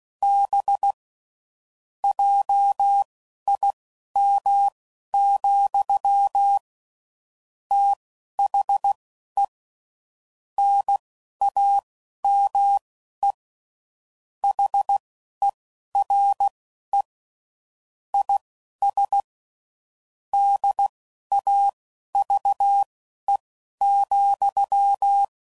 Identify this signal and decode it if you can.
Morse code, heard as a keyed tone, about 16 wpm, spaced as for 11 wpm, 800 Hz. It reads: B JIM, THE NAME HERE IS DAVE,
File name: CW Code Groups @15WPM / QSOs / WD0BGP Spaced